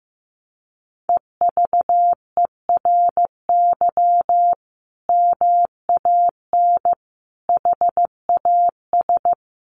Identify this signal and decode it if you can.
Morse code, heard as a keyed tone, about 15 wpm, 700 Hz. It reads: EVERY MAN HAS
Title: Sentences from Top 600 Words 15wpm